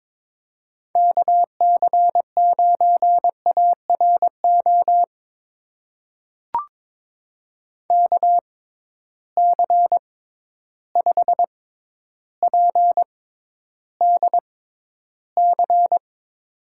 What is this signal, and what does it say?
Morse code, heard as a keyed tone, about 22 wpm, spaced as for 11 wpm, 700 Hz. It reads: KC9ARO KC5PDC
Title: USA Call Signs - 6 Characters - 6x Character Spacing - 22wpm